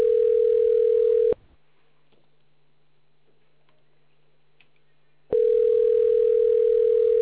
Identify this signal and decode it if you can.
Ringback tone (North America), 440 + 480 Hz.